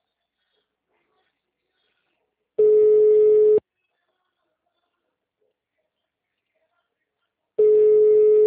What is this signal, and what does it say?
Ringback tone (Europe, 425 Hz).